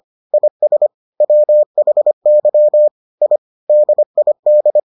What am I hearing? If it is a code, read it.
Morse code, heard as a keyed tone, about 25 wpm, 600 Hz. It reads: IS WHY I DID